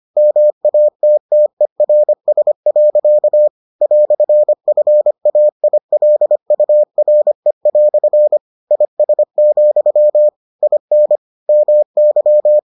Morse code, heard as a keyed tone, about 25 wpm, 600 Hz. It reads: MATTERS. "FAILURE" IS, IN MY